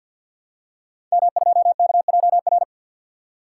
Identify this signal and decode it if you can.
Morse code, heard as a keyed tone, about 46 wpm, 700 Hz. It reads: M2KJR
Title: Code Groups - 5 Letter:Numbers - 50wpm